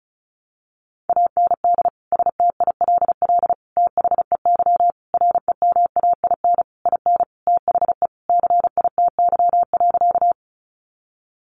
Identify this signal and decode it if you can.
Morse code, heard as a keyed tone, about 34 wpm, 700 Hz. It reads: AND STILL THEY REMAIN IN THE CITY.
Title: Sentences from Top 1200 Words 35wpm